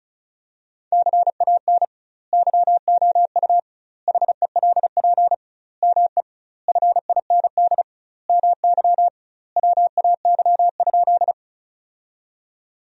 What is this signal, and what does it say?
Morse code, heard as a keyed tone, about 36 wpm, 700 Hz. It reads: CAN YOU HELP ME FIND MY WAY?
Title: 2x Word-Spacing - Sentences from Top 200 Words - 35wpm